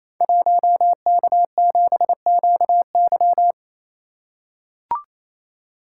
Morse code, heard as a keyed tone, about 27 wpm, 700 Hz. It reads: 1K7QY
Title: Code Groups - 5 Letter:Numbers - 28wpm